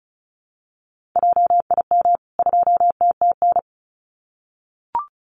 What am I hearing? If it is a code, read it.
Morse code, heard as a keyed tone, about 34 wpm, 700 Hz. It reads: JIM 2TTN
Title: CWT Exchange 35wpm